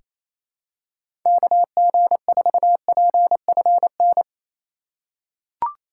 Morse code, heard as a keyed tone, about 28 wpm, 700 Hz. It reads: KG4PFN
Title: USA Call Signs - 6 Characters - 8x Character Spacing - 28wpm